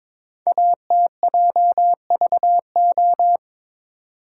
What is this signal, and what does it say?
Morse code, heard as a keyed tone, about 22 wpm, 700 Hz. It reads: ATJVO
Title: Code Groups - 5 Letters - 22wpm